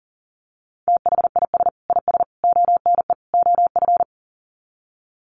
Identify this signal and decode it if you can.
Morse code, heard as a keyed tone, about 40 wpm, 700 Hz. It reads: THIS IS ONE OF
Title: Sets of 4 Words 40wpm